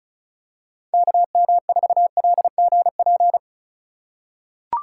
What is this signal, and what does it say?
Morse code, heard as a keyed tone, about 36 wpm, 690 Hz. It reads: KM4LGP
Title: USA Call Signs - 6 Characters - 6x Character Spacing - 35wpm